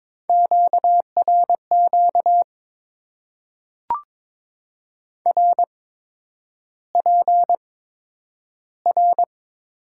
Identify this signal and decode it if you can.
Morse code, heard as a keyed tone, about 22 wpm, 700 Hz. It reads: Q R Q R P R
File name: Top 90 QSO Elements - 8x Character Spacing - 22wpm